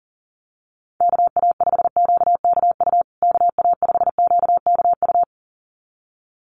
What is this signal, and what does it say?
Morse code, heard as a keyed tone, about 40 wpm, 700 Hz. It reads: KA5QKU KA5QKU
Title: USA Call Signs Repeated 2x 40wpm